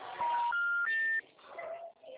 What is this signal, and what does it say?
Special information tone, with noise.